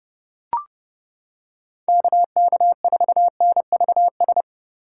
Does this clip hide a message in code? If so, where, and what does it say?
Yes: Morse code from 1.9 to 4.4 s, heard as a keyed tone, about 31 wpm, 700 Hz. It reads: KK4NVS